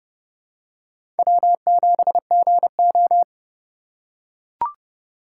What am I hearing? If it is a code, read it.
Morse code, heard as a keyed tone, about 30 wpm, 700 Hz. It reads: W7GO
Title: USA Call Signs - 4 Characters - 2x Character Spacing - 30wpm